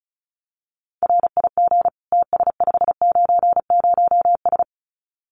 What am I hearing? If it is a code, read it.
Morse code, heard as a keyed tone, about 36 wpm, 700 Hz. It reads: RIG TS590S
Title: Common QSO Phrases Encore 35wpm